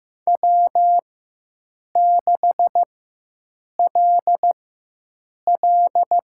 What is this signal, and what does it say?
Morse code, heard as a keyed tone, about 15 wpm, spaced as for 7 wpm, 700 Hz. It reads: W6LL
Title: USA Call Signs - 4 Characters - 4x Character Spacing - 15wpm